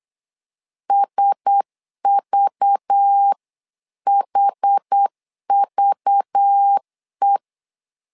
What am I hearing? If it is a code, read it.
Morse code, heard as a keyed tone, about 8 wpm, 800 Hz. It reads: SVHVE
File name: LEZIONI / 5 LBVC / 5 LBV 8wpm 2837